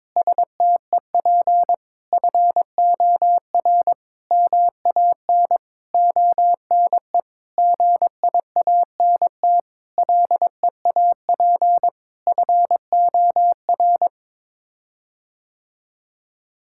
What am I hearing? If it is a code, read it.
Morse code, heard as a keyed tone, about 22 wpm, 700 Hz. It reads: STEP FOR MAN ONE GIANT LEAP FOR
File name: Sets of 9 Words - 22wpm